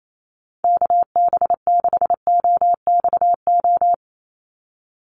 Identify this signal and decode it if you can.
Morse code, heard as a keyed tone, about 28 wpm, 700 Hz. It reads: KB6OXO
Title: USA Call Signs 28wpm